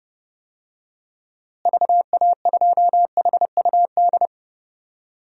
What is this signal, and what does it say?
Morse code, heard as a keyed tone, about 30 wpm, 700 Hz. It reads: VA2HUD
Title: DXCC-100 Call Signs - 6 Characters - Repeated 2x - 30wpm